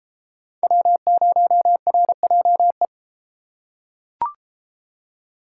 Morse code, heard as a keyed tone, about 33 wpm, 700 Hz. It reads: W0RJE